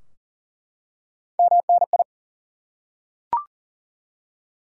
Morse code, heard as a keyed tone, about 40 wpm, 700 Hz. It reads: MNI
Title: Top 180 QSO Elements 40wpm